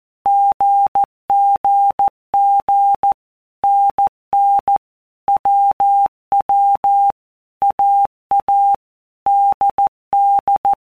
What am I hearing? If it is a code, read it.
Morse code, heard as a keyed tone, about 14 wpm, 800 Hz. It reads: GGG NN WW AA DD
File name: Basic Letter Groups By Blocks / Groups@13 / AWUNDG@13